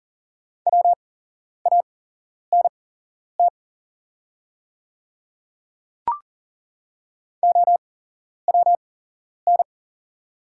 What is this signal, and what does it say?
Morse code, heard as a keyed tone, about 38 wpm, spaced as for 11 wpm, 700 Hz. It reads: WANT OWN
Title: 8x Character Spacing - Top 500 Words - 40wpm